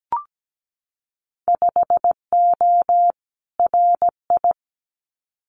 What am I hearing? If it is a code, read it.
Morse code, heard as a keyed tone, about 17 wpm, 700 Hz. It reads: 5O RI
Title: Winter Field Day Exchanges Encore 17wpm